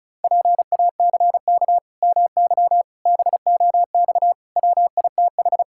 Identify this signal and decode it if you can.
Morse code, heard as a keyed tone, about 36 wpm, 700 Hz. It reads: PACK MY BOX WITH
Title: Pangram Challenge v2 - 35wpm